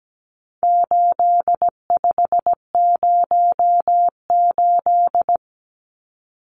Morse code, heard as a keyed tone, about 17 wpm, 700 Hz. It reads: 8508
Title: Numbers - 4-Digit 17wpm